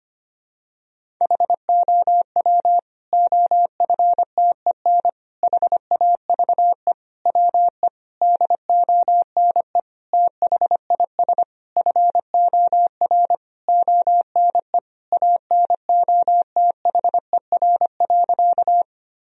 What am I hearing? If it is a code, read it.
Morse code, heard as a keyed tone, about 25 wpm, 700 Hz. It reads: HOW OFTEN HAVE WE DONE THIS FOR ONE ANOTHER.
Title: Sentences from Top 1000 Words - Review Part 1 - 25wpm